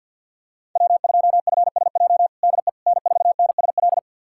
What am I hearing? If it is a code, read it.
Morse code, heard as a keyed tone, about 52 wpm, 700 Hz. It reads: W2FSJ DE N4NSL
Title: USA Callsign DE Callsign - 5 Characters - 50wpm